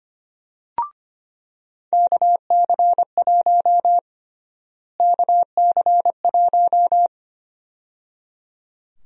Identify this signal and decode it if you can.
Morse code, heard as a keyed tone, about 25 wpm, 700 Hz. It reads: KC1 KC1
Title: USA Call Sign Prefixes - Repeated 2x - 25wpm